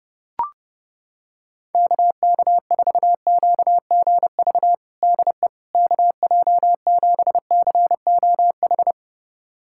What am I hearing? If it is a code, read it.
Morse code, heard as a keyed tone, about 30 wpm, 700 Hz. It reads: KK4QGV DE KJ7COH
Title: USA Callsign DE Callsign 30wpm